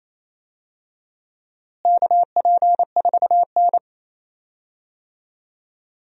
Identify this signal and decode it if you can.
Morse code, heard as a keyed tone, about 28 wpm, 700 Hz. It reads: KP4N
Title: DXCC-100 Call Signs - 4 Characters - Repeated 3x - 28wpm